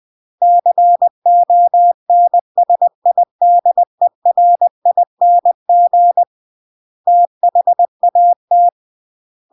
Morse code, heard as a keyed tone, about 20 wpm, 700 Hz. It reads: CONSIDERING THAT
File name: CW / jodorowsky / 82 commandments 20-20